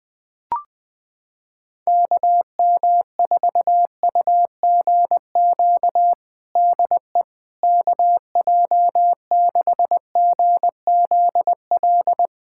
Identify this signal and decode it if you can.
Morse code, heard as a keyed tone, about 20 wpm, 700 Hz. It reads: KM4UGQ DE KJ6GZL
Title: USA Callsign DE Callsign 20wpm